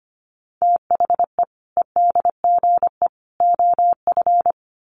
Morse code, heard as a keyed tone, about 25 wpm, 700 Hz. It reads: THE EDGE OF
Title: Sets of 3 Words 25wpm